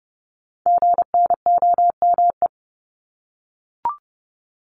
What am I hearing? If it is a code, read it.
Morse code, heard as a keyed tone, about 30 wpm, 700 Hz. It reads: GNOME
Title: Multi-letter Phonemes Repeated 3x 30wpm